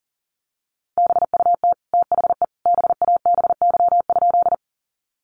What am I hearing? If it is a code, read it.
Morse code, heard as a keyed tone, about 40 wpm, 700 Hz. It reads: BUT THE BABY?
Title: Sentences from Top 1300 Words 40wpm